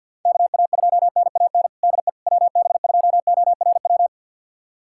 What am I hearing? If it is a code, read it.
Morse code, heard as a keyed tone, about 47 wpm, 700 Hz. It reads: KA2NAN DE WB2ORW
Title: USA Callsign DE Callsign 50wpm